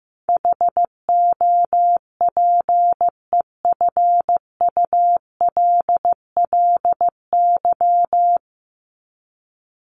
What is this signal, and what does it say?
Morse code, heard as a keyed tone, about 15 wpm, 700 Hz. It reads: HOPEFULLY
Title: Top 1251-1500 Words 15wpm